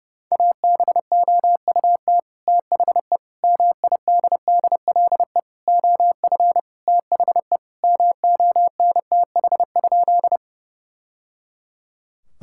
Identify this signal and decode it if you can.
Morse code, heard as a keyed tone, about 32 wpm, 700 Hz. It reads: ABOUT THE MIDDLE OF THE MONTH?
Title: Sentences from Top 1000 Words - Review Part 6 - 30wpm